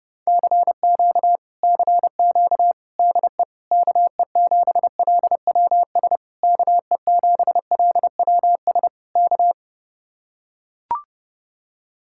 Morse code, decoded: CQ CQ DE KE7LWS KE7LWS K

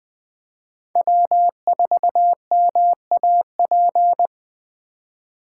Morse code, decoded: W4MAP

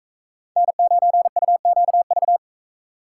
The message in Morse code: N9UQV

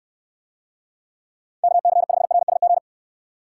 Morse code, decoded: KC5RSG

